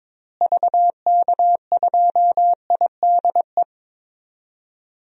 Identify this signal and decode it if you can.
Morse code, heard as a keyed tone, about 22 wpm, 700 Hz. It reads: VK2IDE
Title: DXCC-100 Call Signs - 6 Characters - Repeated 2x - 22wpm